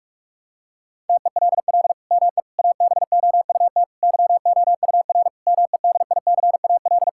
Morse code, heard as a keyed tone, about 41 wpm, 700 Hz. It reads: TELL ME ABOUT YOUR MEDICAL
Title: Sentences from Top 1000 Words - Review Part 5 - 45wpm